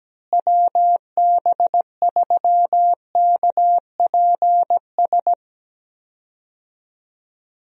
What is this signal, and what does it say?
Morse code, heard as a keyed tone, about 17 wpm, 700 Hz. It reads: WB3KPS
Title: USA Call Signs - 6 Characters - Repeated 2x - 17wpm